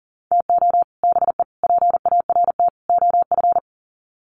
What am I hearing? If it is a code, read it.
Morse code, heard as a keyed tone, about 40 wpm, 700 Hz. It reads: TO BE PART OF
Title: Sets of 4 Words 40wpm